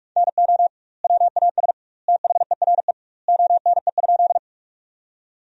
Morse code, heard as a keyed tone, about 46 wpm, 700 Hz. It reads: NO WAS THERE ONE?
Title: 2x Word-Spacing - Sentences from Top 100 Words - 45wpm